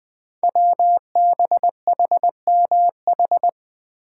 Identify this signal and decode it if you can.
Morse code, heard as a keyed tone, about 20 wpm, 700 Hz. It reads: WBHMH
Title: Code Groups - 5 Letter:Numbers - 20wpm